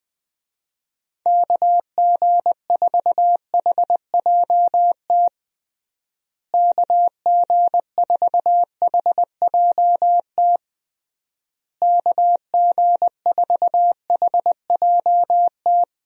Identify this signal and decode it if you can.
Morse code, heard as a keyed tone, about 20 wpm, 700 Hz. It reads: KG4HJT KG4HJT KG4HJT
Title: USA Call Signs - 6 Characters - Repeated 3x - 20wpm